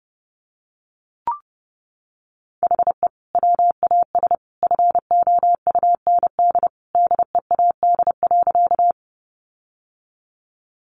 Morse code, heard as a keyed tone, about 30 wpm, 700 Hz. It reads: HE WAS FOUND DEAD.
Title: Sentences from Top 1500 Words 30wpm